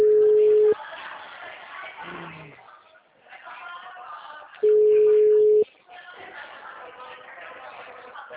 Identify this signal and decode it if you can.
Ringback tone (Europe, 425 Hz), with speech.